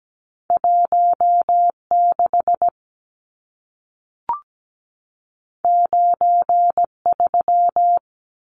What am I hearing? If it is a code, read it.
Morse code, heard as a keyed tone, about 17 wpm, 700 Hz. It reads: 16 93
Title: Numbers - 2-Digit 17wpm